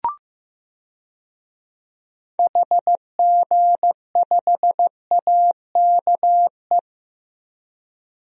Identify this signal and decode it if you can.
Morse code, heard as a keyed tone, about 15 wpm, 700 Hz. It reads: HG5AKE
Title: DXCC-100 Call Signs - 6 Characters - 15wpm